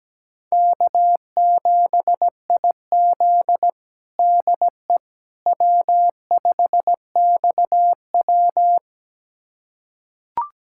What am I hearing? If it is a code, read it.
Morse code, heard as a keyed tone, about 17 wpm, 700 Hz. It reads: K7IZ DE W5XW
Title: USA Callsign DE Callsign - 4 Characters - 17wpm